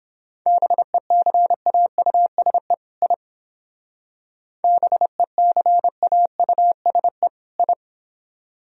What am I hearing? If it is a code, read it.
Morse code, heard as a keyed tone, about 30 wpm, 700 Hz. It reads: BECAUSE I BECAUSE I